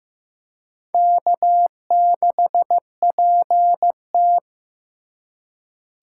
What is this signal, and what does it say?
Morse code, heard as a keyed tone, about 15 wpm, 700 Hz. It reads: K6PT
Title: USA Call Signs - 4 Characters - 2x Character Spacing - 15wpm